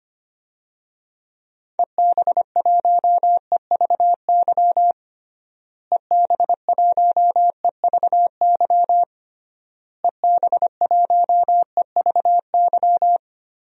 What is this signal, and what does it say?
Morse code, heard as a keyed tone, about 25 wpm, 700 Hz. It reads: EB1EVY EB1EVY EB1EVY
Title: DXCC-100 Call Signs - 6 Characters - Repeated 3x - 25wpm